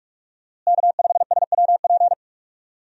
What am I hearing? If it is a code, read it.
Morse code, heard as a keyed tone, about 46 wpm, 700 Hz. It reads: K5SWP
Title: USA Call Signs - 5 Characters - Repeated 2x - 45wpm